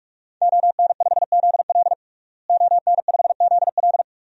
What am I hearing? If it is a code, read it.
Morse code, heard as a keyed tone, about 47 wpm, 700 Hz. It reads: ON5ZL ON5ZL
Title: DXCC-100 Call Signs - 5 Characters - Repeated 2x - 45wpm